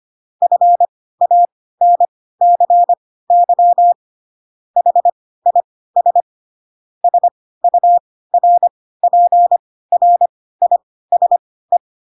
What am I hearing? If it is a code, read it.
Morse code, heard as a keyed tone, about 25 wpm, spaced as for 16 wpm, 700 Hz. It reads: FANCY HIS SURPRISE
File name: Pinocchio / Pinocchio ch3 10 25 ik1hge